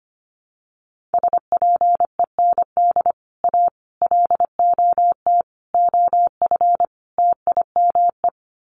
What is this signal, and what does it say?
Morse code, heard as a keyed tone, about 25 wpm, 700 Hz. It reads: SPEND A LOT OF TIME
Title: Sets of 5 Words 25wpm